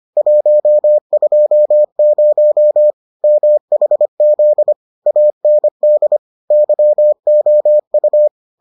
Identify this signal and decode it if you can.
Morse code, heard as a keyed tone, about 25 wpm, 600 Hz. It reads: 120 MHZ AND YOU